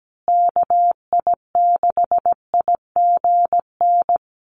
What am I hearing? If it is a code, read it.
Morse code, heard as a keyed tone, about 17 wpm, 700 Hz. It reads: KI6IGN
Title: USA Call Signs Repeated 2x 17wpm